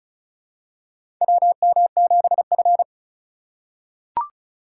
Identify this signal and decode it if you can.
Morse code, heard as a keyed tone, about 36 wpm, 700 Hz. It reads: WM7F
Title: USA Call Signs - 2x Character Spacing - 35wpm